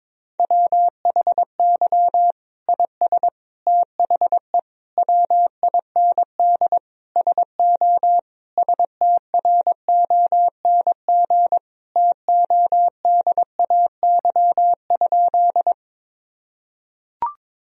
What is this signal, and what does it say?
Morse code, heard as a keyed tone, about 21 wpm, 700 Hz. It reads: WHY IS THE WIND SO STRONG TODAY?
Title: Sentences from Top 1000 Words - Review Part 2 - 22wpm